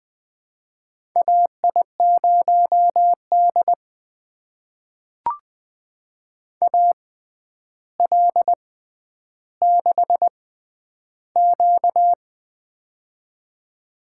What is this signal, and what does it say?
Morse code, heard as a keyed tone, about 20 wpm, spaced as for 10 wpm, 700 Hz. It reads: AI0D AL6Q
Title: USA Call Signs - 4 Characters - 6x Character Spacing - 20wpm